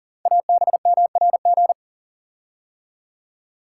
Morse code, heard as a keyed tone, about 38 wpm, 690 Hz. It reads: ABMRG
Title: Code Groups - 5 Letters - 40wpm